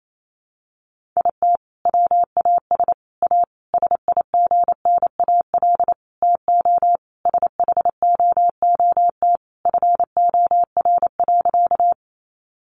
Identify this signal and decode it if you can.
Morse code, heard as a keyed tone, about 28 wpm, 700 Hz. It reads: IT WAS A SIGNAL TO SHOOT FOR.
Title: Sentences from Top 1400 Words 28wpm